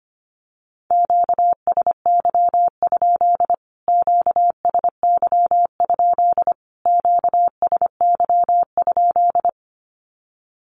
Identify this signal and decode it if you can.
Morse code, heard as a keyed tone, about 25 wpm, 700 Hz. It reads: QSY? QSY? QSY?
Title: Q-Codes Repeated 3x 25wpm